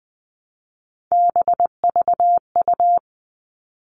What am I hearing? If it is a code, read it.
Morse code, heard as a keyed tone, about 20 wpm, 700 Hz. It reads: BVU
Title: All Cut Numbers - 3-Digit 20wpm